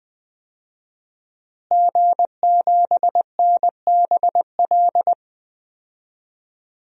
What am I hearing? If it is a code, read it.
Morse code, heard as a keyed tone, about 20 wpm, 700 Hz. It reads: G7NBL
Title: DXCC-100 Call Signs - 5 Characters - Repeated 2x - 20wpm